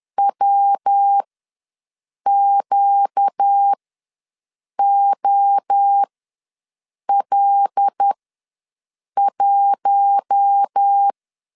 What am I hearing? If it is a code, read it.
Morse code, heard as a keyed tone, about 10 wpm, spaced as for 6 wpm, 800 Hz. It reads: WQOL1
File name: LEZIONI / 5 LBVC / 5 L ws 10wpm 3636